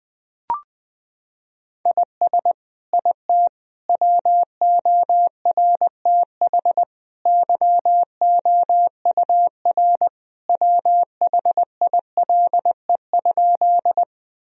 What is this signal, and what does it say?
Morse code, heard as a keyed tone, about 20 wpm, 700 Hz. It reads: IS IT WORTH YOUR WHILE?
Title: Sentences from Top 1000 Words - Review Part 1 - 20wpm